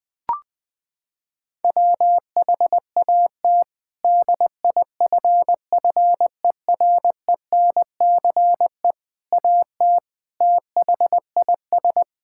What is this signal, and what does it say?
Morse code, heard as a keyed tone, about 21 wpm, 700 Hz. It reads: WHAT DIFFERENCE AT THIS
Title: Sentences with 8-Word Phrases - 20wpm